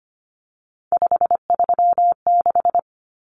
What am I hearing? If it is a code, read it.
Morse code, heard as a keyed tone, about 25 wpm, 700 Hz. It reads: 536
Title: Numbers - 3-Digit 25wpm